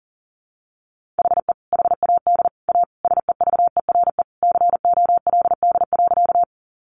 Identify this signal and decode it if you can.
Morse code, heard as a keyed tone, about 40 wpm, 700 Hz. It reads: HE HAD A SEVERE COLD.
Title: Sentences from Top 1400 Words 40wpm